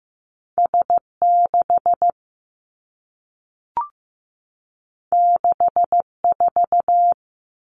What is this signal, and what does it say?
Morse code, heard as a keyed tone, about 15 wpm, 700 Hz. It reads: S6 64
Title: Two Challenging Letter-Number Combinations 15wpm